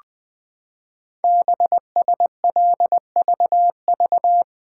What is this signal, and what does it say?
Morse code, heard as a keyed tone, about 20 wpm, 700 Hz. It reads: BSLVV
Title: Code Groups - 5 Letters - 20wpm